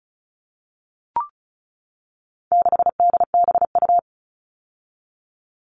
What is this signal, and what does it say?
Morse code, heard as a keyed tone, about 34 wpm, 700 Hz. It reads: 6DBU
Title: All Cut Numbers - 4-Digit 35wpm